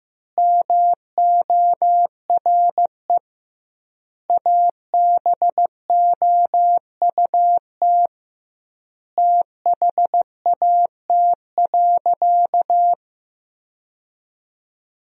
Morse code, heard as a keyed tone, about 15 wpm, 700 Hz. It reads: MORE ABOUT THAT.
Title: 2x Word-Spacing - Sentences from Top 200 Words - 15wpm